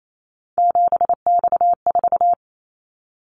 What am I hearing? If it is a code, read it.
Morse code, heard as a keyed tone, about 28 wpm, 700 Hz. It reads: 7X4
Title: Callsign Prefixes for Top 100 DXCC - 28wpm